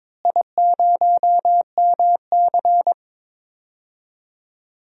Morse code, heard as a keyed tone, about 22 wpm, 700 Hz. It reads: I0MC
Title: DXCC-100 Call Signs - 4 Characters - Repeated 3x - 22wpm